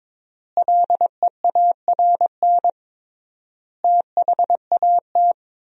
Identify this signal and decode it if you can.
Morse code, heard as a keyed tone, about 22 wpm, 700 Hz. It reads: LEARN THAT
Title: 3x Word-Spacing - Sentences from Top 300 Words - 22wpm